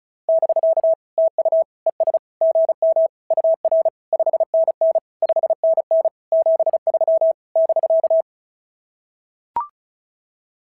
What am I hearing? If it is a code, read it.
Morse code, heard as a keyed tone, about 36 wpm, 650 Hz. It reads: <BK> TU ES GM UR 5NN 5NN 73 <BK>
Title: SOTA QSO Practice 35wpm